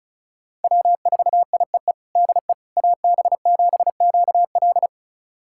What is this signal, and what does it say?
Morse code, heard as a keyed tone, about 34 wpm, 700 Hz. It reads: W4IEE DE AB7QL
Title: USA Callsign DE Callsign - 5 Characters - 35wpm